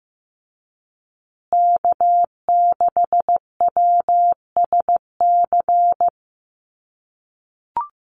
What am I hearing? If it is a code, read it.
Morse code, heard as a keyed tone, about 15 wpm, 700 Hz. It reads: K6WSC
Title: USA Call Signs Repeated 2x 15wpm